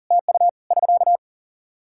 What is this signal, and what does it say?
Morse code, heard as a keyed tone, about 40 wpm, 700 Hz. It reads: TU <SK>
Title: K1USN SST QSO Practice 40wpm